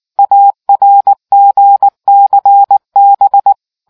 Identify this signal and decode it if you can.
Morse code, heard as a keyed tone, about 19 wpm, 800 Hz. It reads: ARGCB